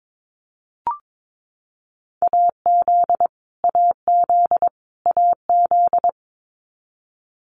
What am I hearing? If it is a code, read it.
Morse code, heard as a keyed tone, about 22 wpm, 700 Hz. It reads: AZ AZ AZ
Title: US States Repeated 3x 22wpm